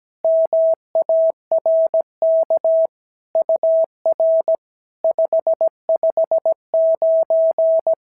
Morse code, heard as a keyed tone, about 17 wpm, 650 Hz. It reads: MARK UR 559
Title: SOTA QSO Practice 17wpm